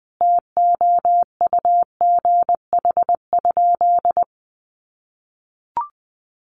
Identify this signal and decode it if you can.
Morse code, heard as a keyed tone, about 20 wpm, 700 Hz. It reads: TOUGH?
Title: Sentences from Top 1200 Words 20wpm